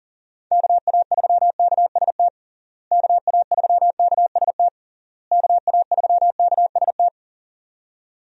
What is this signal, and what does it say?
Morse code, heard as a keyed tone, about 40 wpm, 700 Hz. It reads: KA3KST KA3KST KA3KST
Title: USA Call Signs - 6 Characters - Repeated 3x - 40wpm